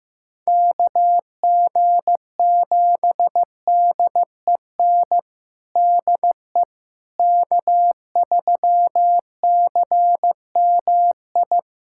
Morse code, heard as a keyed tone, about 16 wpm, 700 Hz. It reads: KG7DEN DE K3CMI
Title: USA Callsign DE Callsign 15wpm